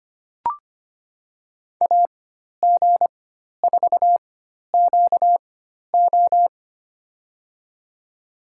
Morse code, heard as a keyed tone, about 25 wpm, spaced as for 12 wpm, 700 Hz. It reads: AG4QO
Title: USA Call Signs - 5 Characters - 4x Character Spacing - 25wpm